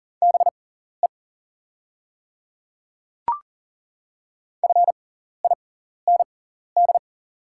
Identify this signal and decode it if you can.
Morse code, heard as a keyed tone, about 38 wpm, spaced as for 14 wpm, 700 Hz. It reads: BE FIND
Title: 6x Character Spacing - Top 200 Words - 40wpm